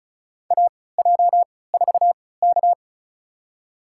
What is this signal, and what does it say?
Morse code, heard as a keyed tone, about 36 wpm, spaced as for 20 wpm, 700 Hz. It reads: AJ4K
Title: USA Call Signs - 4 Characters - 3x Character Spacing - 35wpm